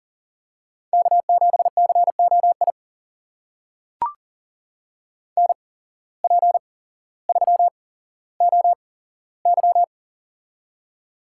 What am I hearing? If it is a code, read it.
Morse code, heard as a keyed tone, about 38 wpm, spaced as for 17 wpm, 700 Hz. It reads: K7COI NP3OY
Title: USA Call Signs - 8x Character Spacing - 40wpm